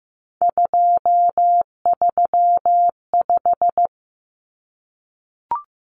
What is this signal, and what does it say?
Morse code, heard as a keyed tone, about 15 wpm, 700 Hz. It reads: 235
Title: Common Cut Numbers - 3-Digit 15wpm